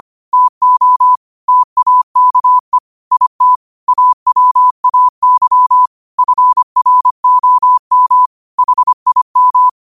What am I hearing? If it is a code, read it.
Morse code, heard as a keyed tone, about 25 wpm, 1000 Hz. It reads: TO TAKE IT AWAY FROM HIM